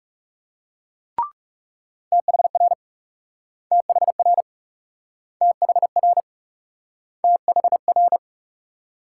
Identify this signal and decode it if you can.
Morse code, heard as a keyed tone, about 38 wpm, 700 Hz. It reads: THR THR THR THR